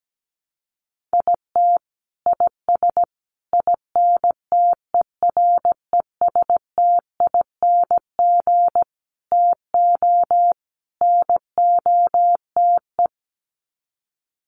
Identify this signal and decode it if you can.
Morse code, heard as a keyed tone, about 17 wpm, 700 Hz. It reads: IT IS INTERESTING TO NOTE
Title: Sets of 5 Words Encore2 17wpm